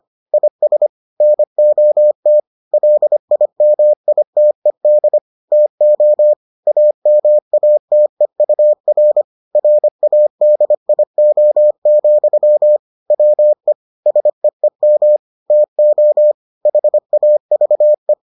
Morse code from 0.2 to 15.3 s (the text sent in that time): IS NOT LIMITED TO AMATEUR RADIO, WE SEEM